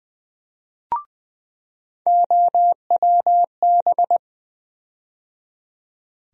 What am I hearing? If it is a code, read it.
Morse code, heard as a keyed tone, about 20 wpm, 700 Hz. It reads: OWB
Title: USA Call Sign Postfixes - 3 Letters 20wpm